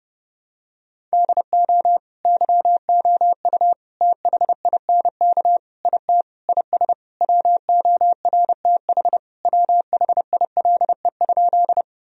Morse code, heard as a keyed tone, about 30 wpm, 700 Hz. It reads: DO YOU THINK IT IS WORTH WHILE?
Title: Sentences from Top 1000 Words - Review Part 1 - 30wpm